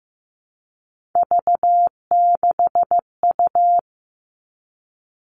Morse code, heard as a keyed tone, about 15 wpm, 700 Hz. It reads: V6U
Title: Three Challenging Letter-Number Combinations 15wpm